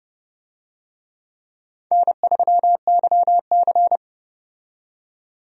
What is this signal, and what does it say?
Morse code, heard as a keyed tone, about 30 wpm, 700 Hz. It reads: N3YC
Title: USA Call Signs - 4 Characters - 30wpm